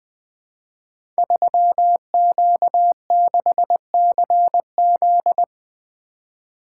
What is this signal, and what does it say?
Morse code, heard as a keyed tone, about 21 wpm, 700 Hz. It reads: 3Q6CZ
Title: Code Groups - 5 Letter:Numbers - 20wpm